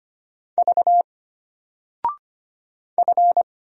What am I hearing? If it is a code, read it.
Morse code, heard as a keyed tone, about 25 wpm, 700 Hz. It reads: V F
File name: Single-Letter-Number-25wpm